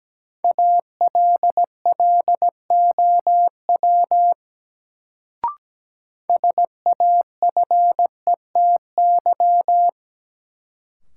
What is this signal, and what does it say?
Morse code, heard as a keyed tone, about 17 wpm, 700 Hz. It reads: ALLOW SAFETY
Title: Sentences from Top 700 Words - New Word Drill - 17wpm